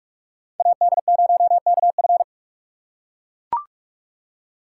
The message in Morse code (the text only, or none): AD0KF